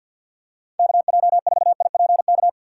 Morse code, heard as a keyed tone, about 48 wpm, 700 Hz. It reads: KJ4IPK
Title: USA Call Signs - 4x Character Spacing - 50wpm